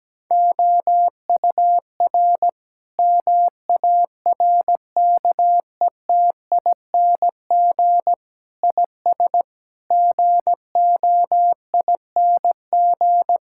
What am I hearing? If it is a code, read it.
Morse code, heard as a keyed tone, about 17 wpm, 700 Hz. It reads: OUR MARKETING IS GOING
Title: Sentences from Top 1000 Words - Review Part 5 - 17wpm